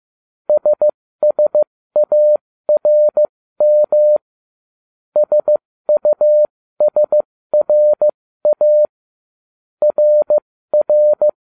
Morse code, heard as a keyed tone, about 15 wpm, spaced as for 13 wpm, 600 Hz. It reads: SSARM SUSRA RR
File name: ubacw05 2